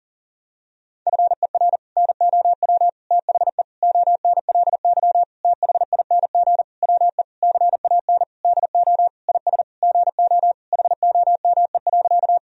Morse code, heard as a keyed tone, about 40 wpm, 700 Hz. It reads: FER NOW THE ONLY THING WE CAN DO IS GO HOME.
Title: Sentences from Top 200 Words and Abbreviations 40wpm